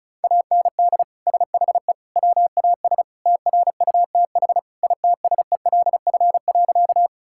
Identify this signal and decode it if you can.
Morse code, heard as a keyed tone, about 36 wpm, 700 Hz. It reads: AND SHE WAS TRUTH ITSELF.
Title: Sentences from Top 1000 Words - Review Part 5 - 35wpm